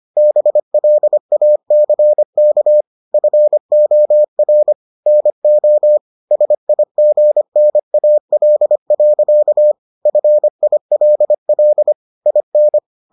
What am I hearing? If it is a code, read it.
Morse code, heard as a keyed tone, about 25 wpm, 600 Hz. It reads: BLACK FOR NO SIGNAL. FILL IN